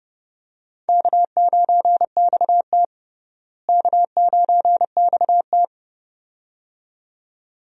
Morse code, heard as a keyed tone, about 32 wpm, 700 Hz. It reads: K9XT K9XT
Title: USA Call Signs - 4 Characters - Repeated 2x - 30wpm